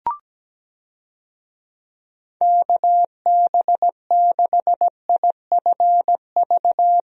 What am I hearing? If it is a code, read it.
Morse code, heard as a keyed tone, about 17 wpm, 700 Hz. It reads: KB6IFV
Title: USA Call Signs - 6 Characters - Repeated 3x - 17wpm